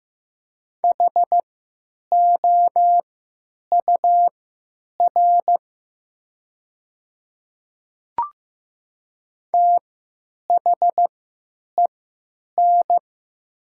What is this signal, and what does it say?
Morse code, heard as a keyed tone, about 15 wpm, spaced as for 9 wpm, 700 Hz. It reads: HOUR THEN
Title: 3x Character Spacing - Top 500 Words - 15wpm